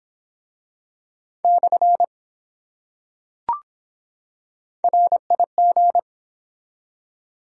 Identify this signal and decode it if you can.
Morse code, heard as a keyed tone, about 23 wpm, 700 Hz. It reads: / RIG